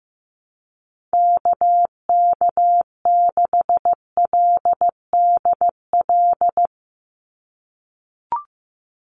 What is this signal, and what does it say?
Morse code, heard as a keyed tone, about 15 wpm, 700 Hz. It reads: KK6LDL
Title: USA Call Signs 15wpm